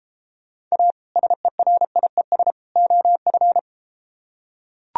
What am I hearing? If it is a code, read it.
Morse code, heard as a keyed tone, about 33 wpm, 700 Hz. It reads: A SERIES OF